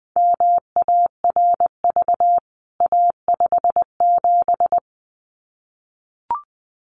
Morse code, heard as a keyed tone, about 20 wpm, 700 Hz. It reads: MARV A57
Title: CWT Exchange 20wpm